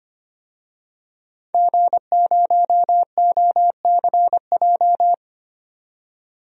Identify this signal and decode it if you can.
Morse code, heard as a keyed tone, about 25 wpm, 700 Hz. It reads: G0OCJ
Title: DXCC-100 Call Signs - 5 Characters - 25wpm